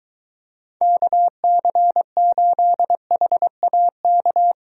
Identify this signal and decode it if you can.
Morse code, heard as a keyed tone, about 23 wpm, 700 Hz. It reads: KC8HAK